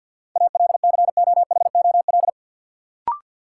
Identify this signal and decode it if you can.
Morse code, heard as a keyed tone, about 53 wpm, 700 Hz. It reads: ALCOHOL
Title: Sentences from Top 1400 Words - New Word Drill - 50wpm